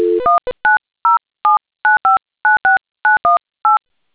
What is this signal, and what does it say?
DTMF, heard as telephone touch tones, tones of about 150 ms. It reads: 19079596918